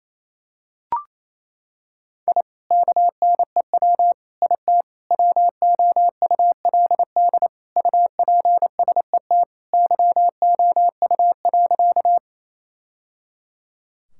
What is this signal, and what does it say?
Morse code, heard as a keyed tone, about 27 wpm, 700 Hz. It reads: I KNEW IT WOULD UPSET YOU.
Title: Sentences from Top 1000 Words - Review Part 5 - 28wpm